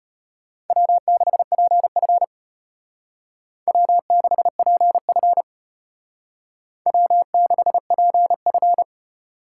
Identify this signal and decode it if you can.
Morse code, heard as a keyed tone, about 32 wpm, 700 Hz. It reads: W6PF W6PF W6PF